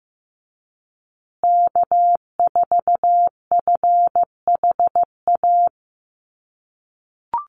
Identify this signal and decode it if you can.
Morse code, heard as a keyed tone, about 15 wpm, 700 Hz. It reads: K4FHA
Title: USA Call Signs Repeated 2x 15wpm